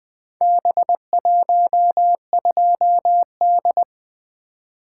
Morse code, heard as a keyed tone, about 20 wpm, 700 Hz. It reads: B12D
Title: Four Letter-Number Combinations - 20wpm